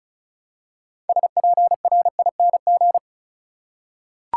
Morse code, heard as a keyed tone, about 36 wpm, 700 Hz. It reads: SPRING
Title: Six Letter Words - 35wpm